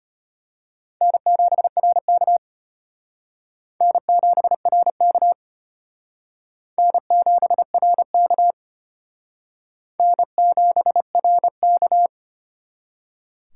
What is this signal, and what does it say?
Morse code, heard as a keyed tone, about 32 wpm, 700 Hz. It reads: N7RK N7RK N7RK N7RK